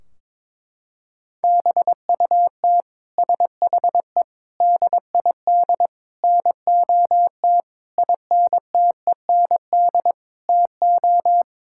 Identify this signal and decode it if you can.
Morse code, heard as a keyed tone, about 22 wpm, 700 Hz. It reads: BUT SHE DID NOT INTEND TO
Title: Sentences from Top 1600 Words 22wpm